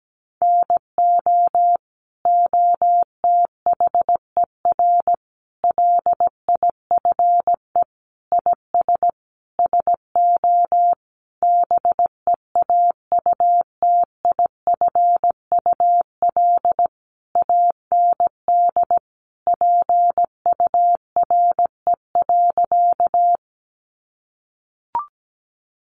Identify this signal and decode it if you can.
Morse code, heard as a keyed tone, about 17 wpm, 700 Hz. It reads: NO OTHER LIFE IS SO BEAUTIFUL AND PURE.
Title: Sentences from Top 1500 Words 17wpm